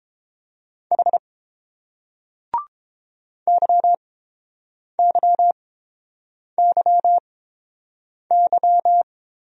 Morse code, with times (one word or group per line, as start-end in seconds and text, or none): 0.9-1.2 H
3.5-9.0 YYYY